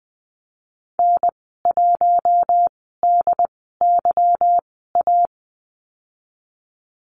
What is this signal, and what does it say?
Morse code, heard as a keyed tone, about 20 wpm, spaced as for 15 wpm, 700 Hz. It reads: N1DYA
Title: 147 Lesson 24g - Letter y - Call Signs 20wpm